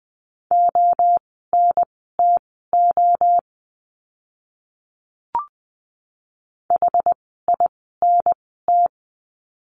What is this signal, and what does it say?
Morse code, heard as a keyed tone, about 20 wpm, spaced as for 15 wpm, 700 Hz. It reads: ONTO HINT
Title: 061 Lesson 11f - Letter h - Words 20wpm